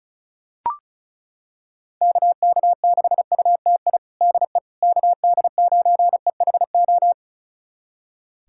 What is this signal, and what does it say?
Morse code, heard as a keyed tone, about 36 wpm, 700 Hz. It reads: KK6UTI DE KD9EHO
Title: USA Callsign DE Callsign 35wpm